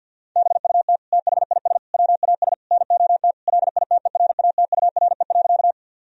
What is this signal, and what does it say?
Morse code, heard as a keyed tone, about 52 wpm, 700 Hz. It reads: BUT THIS WAS NOT LITERATURE.
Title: Sentences from Top 1000 Words - Review Part 3 - 50wpm